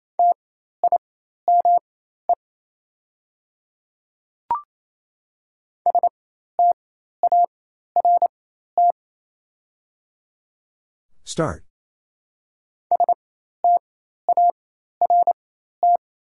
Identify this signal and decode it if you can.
Morse code, heard as a keyed tone, about 28 wpm, spaced as for 13 wpm, 700 Hz. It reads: TIME START START
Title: 4x Character Spacing - Top 300 Words - 28wpm